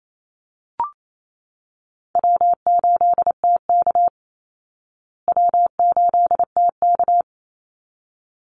Morse code, heard as a keyed tone, about 28 wpm, 700 Hz. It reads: W8TK W8TK
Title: Call Signs Repeated 3x - 28wpm